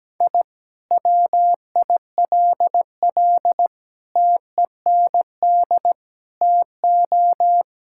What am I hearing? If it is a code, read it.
Morse code, heard as a keyed tone, about 17 wpm, 700 Hz. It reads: I WILL TEND TO
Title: Sentences from Top 1000 Words - Review Part 5 - 17wpm